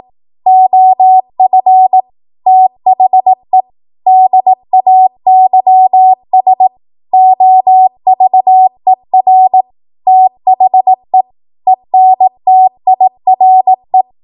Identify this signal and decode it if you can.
Morse code, heard as a keyed tone, about 18 wpm, 750 Hz. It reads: OF THE DAYS OVER THE ENTIRE
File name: Archive / 18 WPM / 18WPM